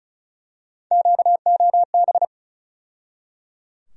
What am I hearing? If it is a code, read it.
Morse code, heard as a keyed tone, about 34 wpm, 700 Hz. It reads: QOB
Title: USA Call Sign Postfixes - 3 Letters 35wpm